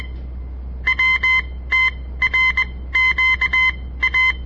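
Morse code, heard as a keyed tone, about 21 wpm, spaced as for 15 wpm, 2050 Hz. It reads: WTRQA
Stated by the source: On-air Morse code practice broadcast by a southern Montana amateur radio association; all code, no read back.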